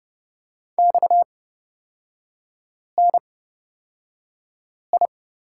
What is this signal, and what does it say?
Morse code, heard as a keyed tone, about 29 wpm, spaced as for 5 wpm, 700 Hz. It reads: XNI
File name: Lesson 37e - Letter x - Three Characters 30wpm